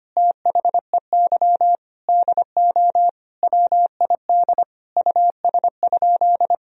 Morse code, heard as a keyed tone, about 25 wpm, 700 Hz. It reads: THEY DO WID US?
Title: Sentences from Top 200 Words and Abbreviations 25wpm